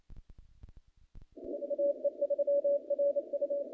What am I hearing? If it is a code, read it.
Morse code, heard as a keyed tone, about 28 wpm, 570 Hz. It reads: AE3RU
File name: contests / naqp-cw-k5tr-op-k5pi- -0100